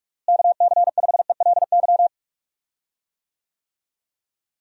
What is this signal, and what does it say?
Morse code, heard as a keyed tone, about 41 wpm, 700 Hz. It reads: KC5ELY
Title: USA Call Signs - 6 Characters - Repeated 2x - 45wpm